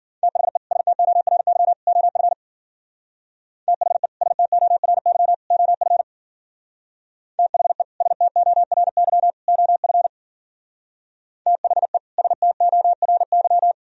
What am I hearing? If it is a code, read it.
Morse code, heard as a keyed tone, about 53 wpm, 700 Hz. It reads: THE STORY OF THE STORY OF THE STORY OF THE STORY